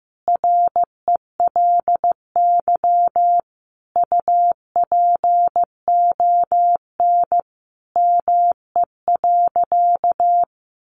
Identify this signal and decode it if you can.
Morse code, heard as a keyed tone, about 14 wpm, 700 Hz. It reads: RELY UPON ME.
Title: Sentences from Top 1200 Words 15wpm